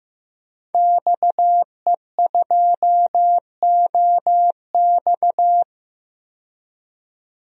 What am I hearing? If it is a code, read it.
Morse code, heard as a keyed tone, about 15 wpm, 700 Hz. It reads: XE2OX
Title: DXCC-100 Call Signs - 5 Characters - 15wpm